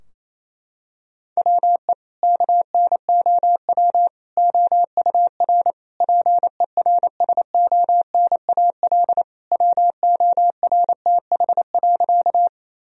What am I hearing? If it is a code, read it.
Morse code, heard as a keyed tone, about 28 wpm, 700 Hz. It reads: WE KNOW OUR PERSONAL WORTH.
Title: Sentences from Top 1000 Words - Review Part 4 - 28wpm